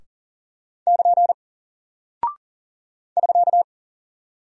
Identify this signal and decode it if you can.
Morse code, heard as a keyed tone, about 40 wpm, 700 Hz. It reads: <KN> <SK>
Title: Single Character (Letter:Number:Punctuation:Prosign) Alt 40wpm